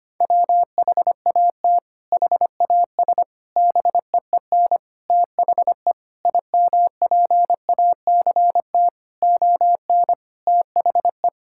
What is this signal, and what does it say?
Morse code, heard as a keyed tone, about 25 wpm, 700 Hz. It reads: WHAT HAS BEEN THE IMPACT ON THE
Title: Sentences from Top 1000 Words - Review Part 6 - 25wpm